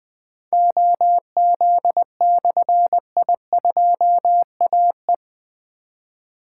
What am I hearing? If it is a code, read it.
Morse code, heard as a keyed tone, about 20 wpm, 700 Hz. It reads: OZ/I2AE
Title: Difficult Call Signs 20wpm